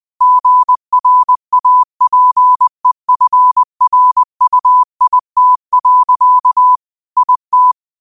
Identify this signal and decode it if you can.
Morse code, heard as a keyed tone, about 20 wpm, 1000 Hz. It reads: GRAPEFRUIT. IT
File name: qotdfiles / 20wpm